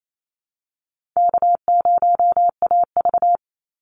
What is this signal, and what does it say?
Morse code, heard as a keyed tone, about 28 wpm, 700 Hz. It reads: K0AV
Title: Easy Call Signs Repeated 3x - 28wpm